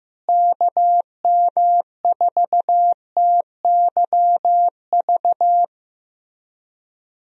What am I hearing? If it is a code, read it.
Morse code, heard as a keyed tone, about 15 wpm, 700 Hz. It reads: KM4TYV
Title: USA Call Signs - 6 Characters - Repeated 3x - 15wpm